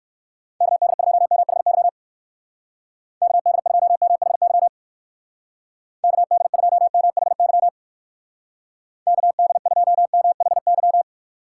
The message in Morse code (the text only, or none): KD2MHY KD2MHY KD2MHY KD2MHY